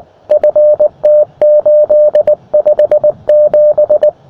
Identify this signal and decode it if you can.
Morse code, heard as a keyed tone, about 19 wpm, 600 Hz, with radio noise. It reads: FT857
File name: attachments / sm7dzd 20m